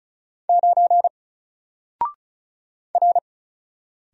Morse code, heard as a keyed tone, about 35 wpm, 700 Hz. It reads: 9 R
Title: Single Letter-Number Alt 35wpm